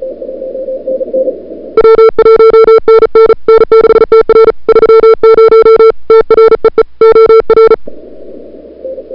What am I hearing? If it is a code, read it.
Morse code, heard as a keyed tone, about 36 wpm, 440 Hz. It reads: W1NN N6TR 30 TREE OR
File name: n6tr / Fab2007CWSprintHour1